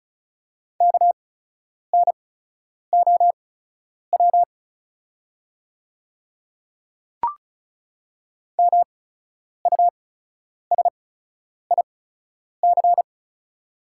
Morse code, heard as a keyed tone, about 36 wpm, spaced as for 10 wpm, 700 Hz. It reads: KNOW MUSIC
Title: 8x Character Spacing - Top 500 Words - 35wpm